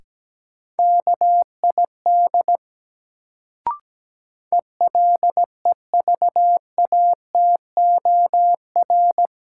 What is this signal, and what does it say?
Morse code, heard as a keyed tone, about 17 wpm, 700 Hz. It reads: KID ELEVATOR
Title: Sentences from Top 1600 Words - New Word Drill - 17wpm